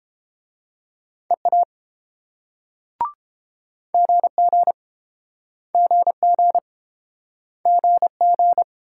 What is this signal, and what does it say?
Morse code, heard as a keyed tone, about 30 wpm, 690 Hz. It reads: EA GG GG GG